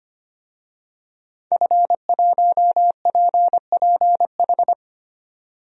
Morse code, heard as a keyed tone, about 25 wpm, 700 Hz. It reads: F1PPH
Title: DXCC-100 Call Signs - 5 Characters - 25wpm